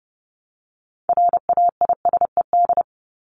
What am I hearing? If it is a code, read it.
Morse code, heard as a keyed tone, about 30 wpm, 700 Hz. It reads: RAISED
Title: ED Suffix - Top 100 Words with Base Words 30wpm